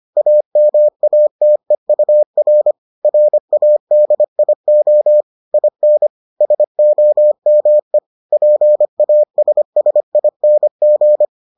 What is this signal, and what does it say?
Morse code, heard as a keyed tone, about 25 wpm, 600 Hz. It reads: AMATEUR RADIO IN SOME PASSING